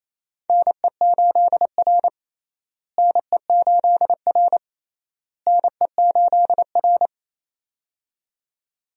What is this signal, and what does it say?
Morse code, heard as a keyed tone, about 27 wpm, 700 Hz. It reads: NE8R NE8R NE8R
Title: USA Call Signs - 4 Characters - Repeated 3x - 28wpm